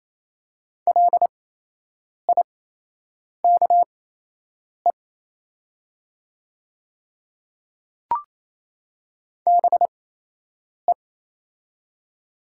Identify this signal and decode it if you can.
Morse code, heard as a keyed tone, about 28 wpm, spaced as for 8 wpm, 700 Hz. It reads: LIKE BE
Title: 8x Character Spacing - Top 300 Words - 28wpm